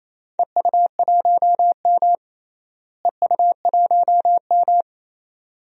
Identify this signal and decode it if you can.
Morse code, heard as a keyed tone, about 27 wpm, 700 Hz. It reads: EU1M EU1M
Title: DXCC-100 Call Signs - 4 Characters - Repeated 2x - 28wpm